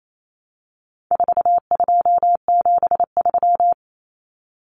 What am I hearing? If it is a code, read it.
Morse code, heard as a keyed tone, about 28 wpm, 700 Hz. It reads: 4273
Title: Numbers - 4-Digit 28wpm